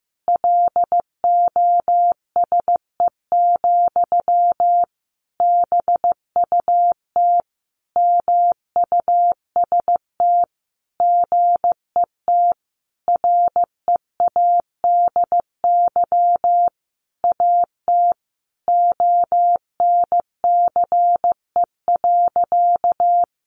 Morse code, heard as a keyed tone, about 15 wpm, 700 Hz. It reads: LOSE, BUT MUST GET READY AT ONCE.
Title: Sentences from Top 1100 Words 15wpm